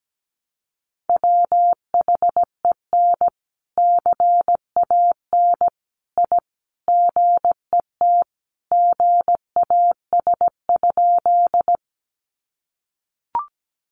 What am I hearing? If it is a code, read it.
Morse code, heard as a keyed tone, about 17 wpm, 700 Hz. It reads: WHEN CAN I GET GAS?
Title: Sentences from Top 1100 Words 17wpm